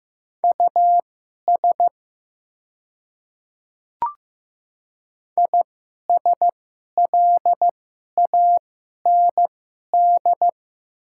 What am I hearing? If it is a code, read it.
Morse code, heard as a keyed tone, about 15 wpm, spaced as for 11 wpm, 690 Hz. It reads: US ISLAND
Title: 2x Character Spacing - Top 500 Words - 15wpm